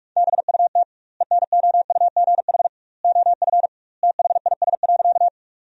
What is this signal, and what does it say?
Morse code, heard as a keyed tone, about 42 wpm, 700 Hz. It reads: BUT ENOUGH OF THIS.
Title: 2x Word-Spacing - Sentences from Top 400 Words - 45wpm